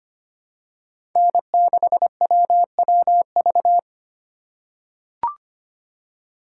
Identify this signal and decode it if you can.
Morse code, heard as a keyed tone, about 25 wpm, 700 Hz. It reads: N6WWV
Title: USA Call Signs - 5 Characters - 3x Character Spacing - 25wpm